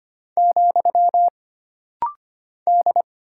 Morse code, heard as a keyed tone, about 25 wpm, 700 Hz. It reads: , D
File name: Single Character (Letter:Number:Punctuation:Prosign) - Rapid-Fire - 25wpm